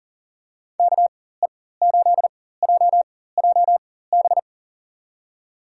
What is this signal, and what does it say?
Morse code, heard as a keyed tone, about 37 wpm, spaced as for 18 wpm, 700 Hz. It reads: KE8JJB
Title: USA Call Signs - 6 Characters - 4x Character Spacing - 40wpm